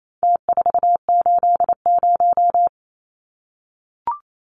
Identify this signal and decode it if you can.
Morse code, heard as a keyed tone, about 28 wpm, 700 Hz. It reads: T480
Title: Numbers - Contest Style - 4-Digit 28wpm